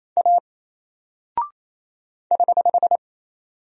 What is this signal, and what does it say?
Morse code, heard as a keyed tone, about 27 wpm, 700 Hz. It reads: A <HH>
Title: Single Character (Letter:Number:Punctuation:Prosign) 28wpm